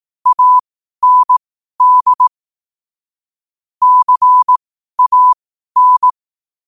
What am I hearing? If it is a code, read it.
Morse code, heard as a keyed tone, about 18 wpm, spaced as for 12 wpm, 1000 Hz. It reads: AND CAN
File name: qotdfiles / 12wpm